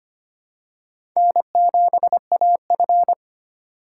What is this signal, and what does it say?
Morse code, heard as a keyed tone, about 25 wpm, 700 Hz. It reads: N7AF
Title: USA Call Signs - 4 Characters - 2x Character Spacing - 25wpm